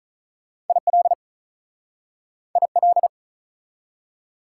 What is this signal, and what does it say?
Morse code, heard as a keyed tone, about 46 wpm, 690 Hz. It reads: IL IL